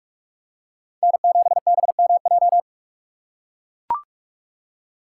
Morse code, heard as a keyed tone, about 44 wpm, 700 Hz. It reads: N7BMJ
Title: USA Call Signs - 5 Characters - 6x Character Spacing - 45wpm